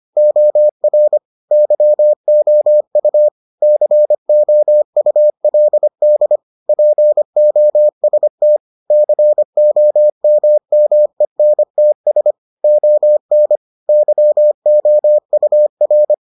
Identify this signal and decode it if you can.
Morse code, heard as a keyed tone, about 25 wpm, 600 Hz. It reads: OR YOU COULD POST COMMENTS ON YOUR